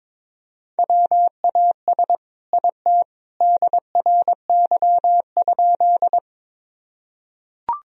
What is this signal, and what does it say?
Morse code, heard as a keyed tone, about 22 wpm, 700 Hz. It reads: WAS IT DRY?
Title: Sentences from Top 1000 Words - Review Part 3 - 22wpm